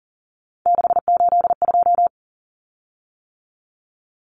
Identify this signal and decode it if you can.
Morse code, heard as a keyed tone, about 40 wpm, 700 Hz. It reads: 682
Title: Common Cut Numbers - 3-Digit 40wpm